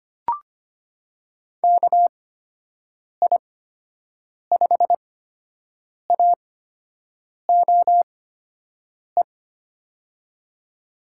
Morse code, heard as a keyed tone, about 25 wpm, spaced as for 7 wpm, 700 Hz. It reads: KI5AOE
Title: USA Call Signs - 6 Characters - 8x Character Spacing - 25wpm